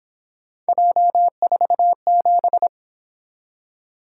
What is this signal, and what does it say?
Morse code, heard as a keyed tone, about 26 wpm, 700 Hz. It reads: J47
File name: Speed-Racing - Three Letter-Number Combinations - 22wpm